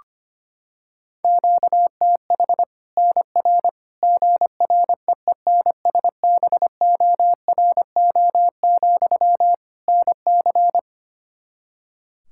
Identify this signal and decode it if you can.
Morse code, heard as a keyed tone, about 25 wpm, 700 Hz. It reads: QTH NR GREENSBORO, NC
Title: Practice QSO - Exchange 1 - Beginning - 25wpm